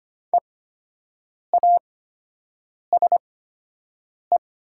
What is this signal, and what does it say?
Morse code, heard as a keyed tone, about 25 wpm, spaced as for 7 wpm, 700 Hz. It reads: EASE
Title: 8x Character Spacing - Top 300 Words - 25wpm